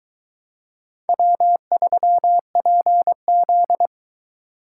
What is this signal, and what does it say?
Morse code, heard as a keyed tone, about 20 wpm, 700 Hz. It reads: W3PZ